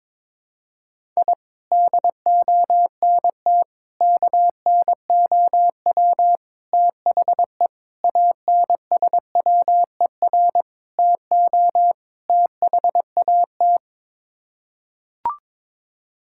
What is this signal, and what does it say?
Morse code, heard as a keyed tone, about 22 wpm, 700 Hz. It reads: I DONT KNOW THE ANSWER TO THAT
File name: practice / 22wpm / Sets of 7 Words - 22wpm